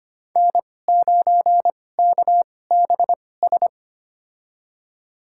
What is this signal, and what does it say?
Morse code, heard as a keyed tone, about 25 wpm, spaced as for 18 wpm, 700 Hz. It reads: N9KBS
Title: USA Call Signs - 5 Characters - 2x Character Spacing - 25wpm